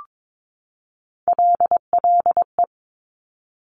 Morse code, heard as a keyed tone, about 22 wpm, 700 Hz. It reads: LLE
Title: Top Three Letter Combinations in Words 22wpm